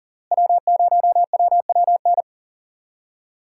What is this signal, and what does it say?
Morse code, heard as a keyed tone, about 37 wpm, 700 Hz. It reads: W0WWN
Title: USA Call Signs - 5 Characters - Repeated 3x - 40wpm